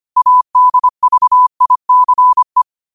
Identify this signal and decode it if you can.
Morse code, heard as a keyed tone, about 25 wpm, 1000 Hz. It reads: ADVICE
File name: qotdfiles / 25wpm